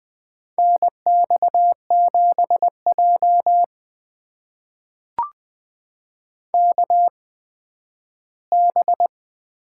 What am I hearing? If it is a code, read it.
Morse code, heard as a keyed tone, about 21 wpm, spaced as for 12 wpm, 700 Hz. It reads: NX7J KB
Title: USA Call Signs - 8x Character Spacing - 20wpm